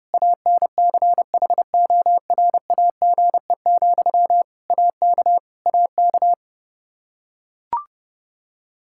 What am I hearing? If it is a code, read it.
Morse code, heard as a keyed tone, about 30 wpm, 700 Hz. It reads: ANCHORAGE, AK AK
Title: Practice QSO - Exchange 1 - Beginning - 30wpm